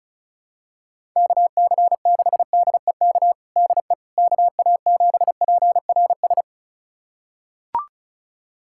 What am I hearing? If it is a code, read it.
Morse code, heard as a keyed tone, about 34 wpm, 700 Hz. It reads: KC6DEK DE KA7PRS
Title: USA Callsign DE Callsign - 6 Characters - 35wpm